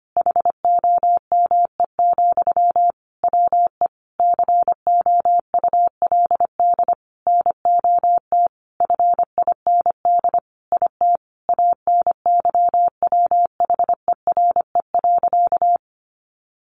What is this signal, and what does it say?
Morse code, heard as a keyed tone, about 25 wpm, 700 Hz. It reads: HOME, WE COULD NOT FIND IT ANYWHERE.
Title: Sentences from Top 1400 Words 25wpm